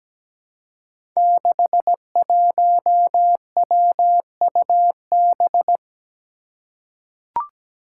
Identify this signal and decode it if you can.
Morse code, heard as a keyed tone, about 17 wpm, 700 Hz. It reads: 61WUB
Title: Five Challenging Letter-Number Combinations 17wpm